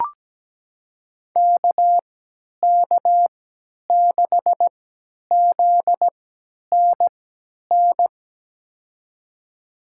Morse code, heard as a keyed tone, about 17 wpm, spaced as for 10 wpm, 700 Hz. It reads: KK6ZNN
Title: USA Call Signs - 3x Character Spacing - 17wpm